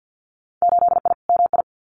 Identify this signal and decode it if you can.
Morse code, heard as a keyed tone, about 50 wpm, 700 Hz. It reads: 8I MI
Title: Winter Field Day Exchanges Encore 50wpm